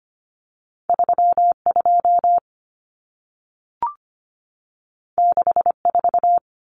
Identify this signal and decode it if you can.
Morse code, heard as a keyed tone, about 25 wpm, 700 Hz. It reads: 32 64
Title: Common Cut Numbers - 2-Digit 25wpm